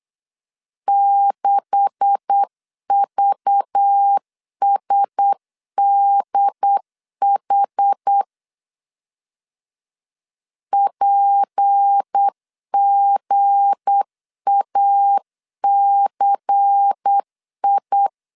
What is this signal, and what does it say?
Morse code, heard as a keyed tone, about 8 wpm, 800 Hz. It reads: 6VSDH PGACI
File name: CW / DET / 1 9 1500C S A 8wpm 5637